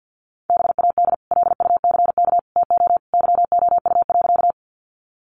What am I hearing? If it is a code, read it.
Morse code, heard as a keyed tone, about 50 wpm, 700 Hz. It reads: BAD LUCK TO YOU.
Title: Sentences from Top 1500 Words 50wpm